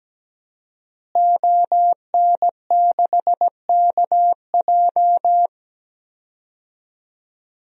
Morse code, heard as a keyed tone, about 17 wpm, 700 Hz. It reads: ON6KJ
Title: DXCC-100 Call Signs - 5 Characters - Repeated 2x - 17wpm